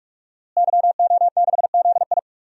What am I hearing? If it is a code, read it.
Morse code, heard as a keyed tone, about 46 wpm, 700 Hz. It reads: YO6ZI
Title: DXCC-100 Call Signs - 5 Characters - Repeated 2x - 45wpm